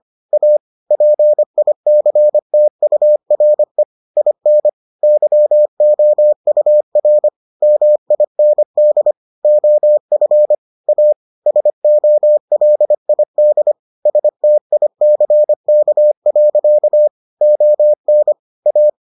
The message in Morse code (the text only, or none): A PICTURE IN YOUR MIND OF A SOLID STICK. ON A